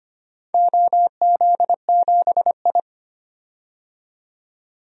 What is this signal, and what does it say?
Morse code, heard as a keyed tone, about 25 wpm, 700 Hz. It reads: OZ7I
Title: DXCC-100 Call Signs - 4 Characters - 25wpm